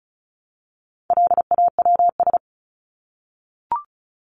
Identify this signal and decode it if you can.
Morse code, heard as a keyed tone, about 34 wpm, 700 Hz. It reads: LAWS
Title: ES:S Suffix - Top 100 Words with Base Words 35wpm